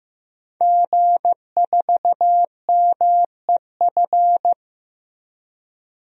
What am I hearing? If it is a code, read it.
Morse code, heard as a keyed tone, about 15 wpm, 700 Hz. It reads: G4MEF
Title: DXCC-100 Call Signs - 5 Characters - Repeated 2x - 15wpm